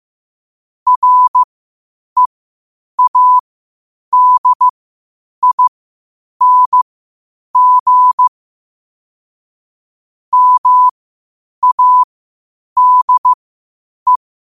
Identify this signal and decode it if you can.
Morse code, heard as a keyed tone, about 15 wpm, spaced as for 8 wpm, 1000 Hz. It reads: READING MADE
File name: qotdfiles / 8wpm